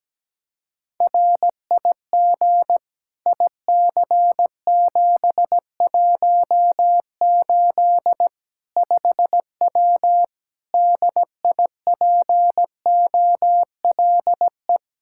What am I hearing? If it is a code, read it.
Morse code, heard as a keyed tone, about 17 wpm, 700 Hz. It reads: RIG IC718 5W DIPOLE
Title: K1USN SST QSO Practice 17wpm